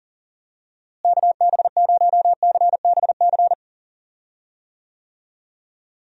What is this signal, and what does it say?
Morse code, heard as a keyed tone, about 40 wpm, 700 Hz. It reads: KB0CBC
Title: USA Call Signs - 6 Characters - Repeated 2x - 40wpm